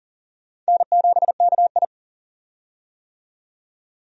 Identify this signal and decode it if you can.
Morse code, heard as a keyed tone, about 40 wpm, 700 Hz. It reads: N7KI
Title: USA Call Signs - 4 Characters - 40wpm